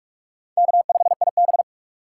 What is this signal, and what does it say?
Morse code, heard as a keyed tone, about 46 wpm, 700 Hz. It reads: K5IB
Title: USA Call Signs - 4 Characters - 6x Character Spacing - 45wpm